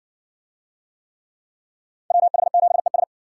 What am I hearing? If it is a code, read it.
Morse code, heard as a keyed tone, about 58 wpm, 700 Hz. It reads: WH7ES